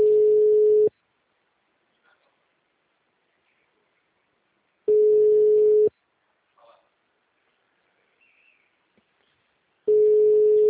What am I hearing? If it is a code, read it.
Ringback tone (Europe, 425 Hz).
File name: out- -2017- - - .3567